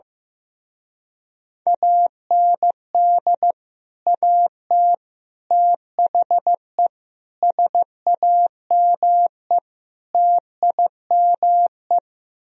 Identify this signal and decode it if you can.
Morse code, heard as a keyed tone, about 15 wpm, 700 Hz. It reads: AND AT THE SAME TIME
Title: Speed-Racing - Sets of 5 Words 15wpm